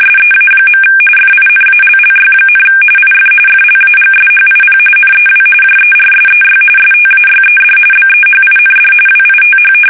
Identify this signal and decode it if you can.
RTTY, about 45.45 baud. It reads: SEA. FINALLY, IF FLAGSHIP NEW JERSEY STAYED BEHIND WITH
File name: fsk850 / battleshipnj